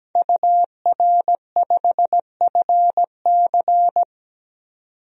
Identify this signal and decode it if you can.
Morse code, heard as a keyed tone, about 17 wpm, 700 Hz. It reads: UR5FC
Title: DXCC-100 Call Signs - 5 Characters - 17wpm